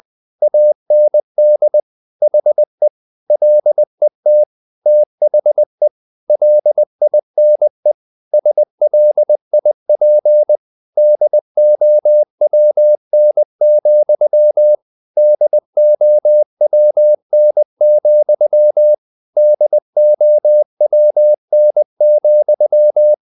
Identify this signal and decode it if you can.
Morse code, heard as a keyed tone, about 20 wpm, 600 Hz. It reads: AND HE LET THE LINE SLIP DOWN, DOWN, DOWN,
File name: the old man and the sea / the old man and the sea part 1 20wpm